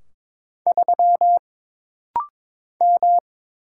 Morse code, heard as a keyed tone, about 22 wpm, 700 Hz. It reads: 3 M
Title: Single Letter-Number Alt - Rapid-Fire 22wpm